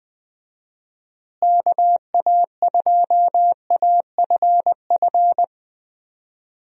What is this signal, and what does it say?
Morse code, heard as a keyed tone, about 21 wpm, 700 Hz. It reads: KA2AFF
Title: USA Call Signs - 6 Characters - Repeated 3x - 20wpm